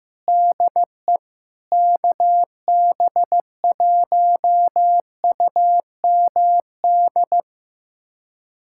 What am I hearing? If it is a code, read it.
Morse code, heard as a keyed tone, about 16 wpm, 700 Hz. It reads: DE KB1UMD
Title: USA Callsign DE Callsign - 6 Characters - 15wpm